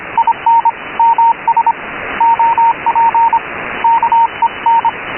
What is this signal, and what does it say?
Morse code, heard as a keyed tone, about 25 wpm, 920 Hz, with radio noise. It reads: IN MS OP KEN
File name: Radio / SDR Audio / gqrx